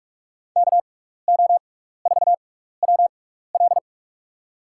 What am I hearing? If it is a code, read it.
Morse code, heard as a keyed tone, about 42 wpm, spaced as for 15 wpm, 700 Hz. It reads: KO4WL
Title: USA Call Signs - 5 Characters - 6x Character Spacing - 45wpm